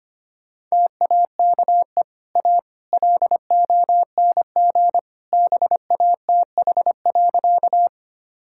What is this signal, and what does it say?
Morse code, heard as a keyed tone, about 25 wpm, 690 Hz. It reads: TAKE A LONG BATH.
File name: Easy Sentences with Hints - 25wpm